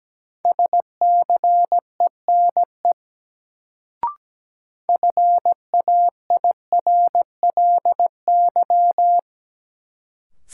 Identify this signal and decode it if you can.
Morse code, heard as a keyed tone, about 17 wpm, 700 Hz. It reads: SCENE FAIRLY
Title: Sentences from Top 1100 Words - New Word Drill - 17wpm